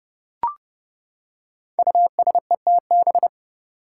Morse code, heard as a keyed tone, about 30 wpm, 700 Hz. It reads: USETB
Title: Five Challenging Letter-Number Combinations 30wpm